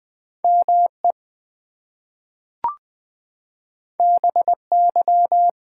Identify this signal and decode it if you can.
Morse code, heard as a keyed tone, about 20 wpm, 700 Hz. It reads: ME BY
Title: Two Letter Words - 20wpm